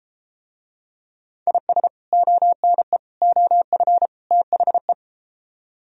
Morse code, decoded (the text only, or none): IS ONE OF THE